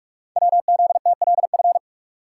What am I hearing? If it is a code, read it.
Morse code, heard as a keyed tone, about 44 wpm, 700 Hz. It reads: WZTLF
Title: Code Groups - 5 Letters - 45wpm